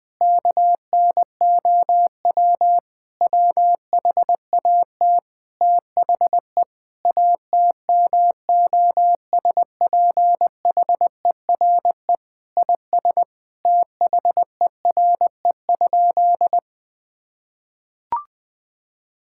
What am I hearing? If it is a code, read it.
Morse code, heard as a keyed tone, about 20 wpm, 700 Hz. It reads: KNOW WHAT THE ATMOSPHERE IS THERE?
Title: Sentences from Top 1600 Words 20wpm